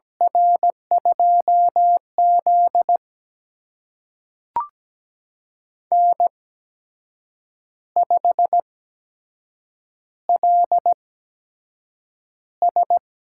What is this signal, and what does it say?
Morse code, heard as a keyed tone, about 17 wpm, spaced as for 7 wpm, 700 Hz. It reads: R2Z N5LS
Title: USA Call Signs - 4 Characters - 8x Character Spacing - 17wpm